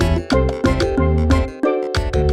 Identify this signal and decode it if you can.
There is no code.